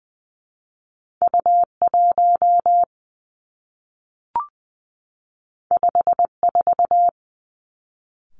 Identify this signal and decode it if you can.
Morse code, heard as a keyed tone, about 20 wpm, 700 Hz. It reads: U 1 5 4
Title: Two Challenging Letter-Number Combinations 20wpm